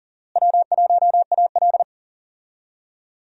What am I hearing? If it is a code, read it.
Morse code, heard as a keyed tone, about 40 wpm, 700 Hz. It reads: W1AL
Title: USA Call Signs - 4 Characters - Repeated 2x - 40wpm